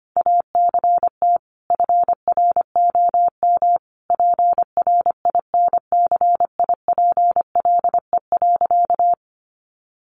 Morse code, heard as a keyed tone, about 25 wpm, 700 Hz. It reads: ACT FROM PRINCIPLE.
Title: Sentences from Top 1400 Words 25wpm